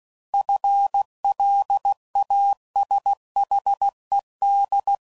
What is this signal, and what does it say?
Morse code, heard as a keyed tone, about 16 wpm, 800 Hz. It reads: FLASHED
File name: Prefixes / FLs & THs / FL Runs